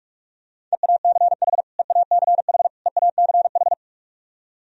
Morse code, heard as a keyed tone, about 45 wpm, 700 Hz. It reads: EACH EACH EACH
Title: Top 100 Words Repeated 45wpm